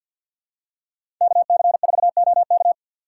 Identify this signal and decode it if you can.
Morse code, heard as a keyed tone, about 52 wpm, 700 Hz. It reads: KC4OK
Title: USA Call Signs - 5 Characters - 3x Character Spacing - 50wpm